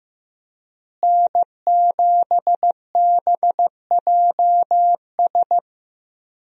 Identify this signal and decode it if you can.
Morse code, heard as a keyed tone, about 16 wpm, 700 Hz. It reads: N7BJS
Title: USA Call Signs - 5 Characters - Repeated 2x - 15wpm